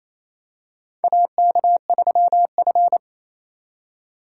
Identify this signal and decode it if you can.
Morse code, heard as a keyed tone, about 28 wpm, 700 Hz. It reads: AK3F
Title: USA Call Signs - 4 Characters - 3x Character Spacing - 28wpm